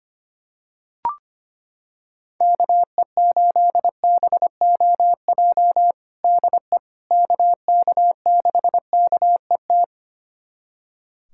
Morse code, heard as a keyed tone, about 25 wpm, 700 Hz. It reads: KE8BOJ DE KK6KET
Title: USA Callsign DE Callsign - 6 Characters - 25wpm